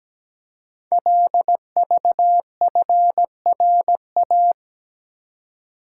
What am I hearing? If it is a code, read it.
Morse code, heard as a keyed tone, about 17 wpm, 700 Hz. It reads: LVFRA
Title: Code Groups - 5 Letters - 17wpm